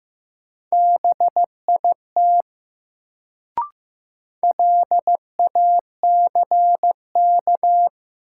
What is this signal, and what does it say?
Morse code, heard as a keyed tone, about 15 wpm, 700 Hz. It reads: BIT LACK
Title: Sets of 3 Words - New Word Drill - 15wpm